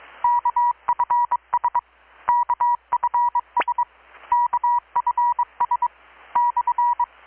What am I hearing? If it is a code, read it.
Morse code, heard as a keyed tone, about 22 wpm, 960 Hz. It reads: KFS KFS KFS /
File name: blog / swl / 6-abril-2019-12695-USB-KPS